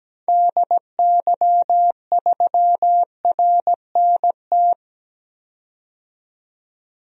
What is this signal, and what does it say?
Morse code, heard as a keyed tone, about 17 wpm, 700 Hz. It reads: DY3RNT
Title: DXCC-100 Call Signs - 6 Characters - 17wpm